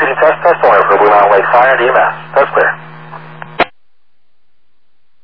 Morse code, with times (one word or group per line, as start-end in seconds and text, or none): none